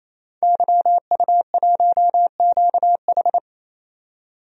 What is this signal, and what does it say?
Morse code, heard as a keyed tone, about 28 wpm, 700 Hz. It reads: YU1QH
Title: DXCC-100 Call Signs - 5 Characters - 28wpm